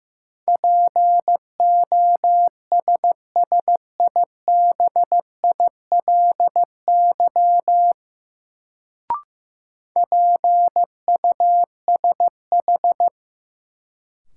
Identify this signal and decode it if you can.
Morse code, heard as a keyed tone, about 15 wpm, 700 Hz. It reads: POSSIBILY PUSH
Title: Sentences from Top 1300 Words - New Word Drill - 15wpm